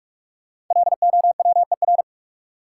Morse code, heard as a keyed tone, about 43 wpm, 700 Hz. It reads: LOWER
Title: Sentences from Top 700 Words - New Word Drill - 45wpm